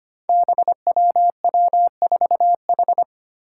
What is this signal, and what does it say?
Morse code, heard as a keyed tone, about 25 wpm, 700 Hz. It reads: BWW4H
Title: Five Challenging Letter-Number Combinations 25wpm